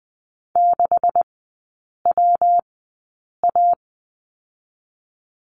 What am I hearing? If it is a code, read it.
Morse code, heard as a keyed tone, about 20 wpm, spaced as for 8 wpm, 700 Hz. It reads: 6WA
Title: 138 Lesson 23e - Letter f - Three Characters 20wpm